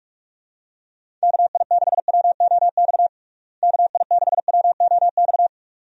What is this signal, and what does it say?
Morse code, heard as a keyed tone, about 46 wpm, 700 Hz. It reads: KI6WOX KI6WOX
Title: USA Call Signs - 6 Characters - Repeated 2x - 45wpm